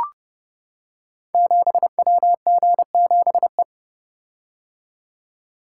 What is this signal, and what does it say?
Morse code, heard as a keyed tone, about 30 wpm, 700 Hz. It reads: 7WG7E
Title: Code Groups - 5 Letter:Numbers - 30wpm